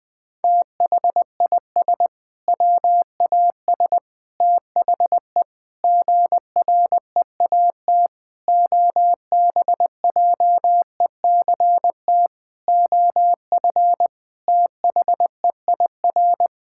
Morse code, heard as a keyed tone, about 20 wpm, 700 Hz. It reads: THIS WAS THE GREAT OBJECT OF THEIR
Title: Sentences from Top 1000 Words - Review Part 5 - 20wpm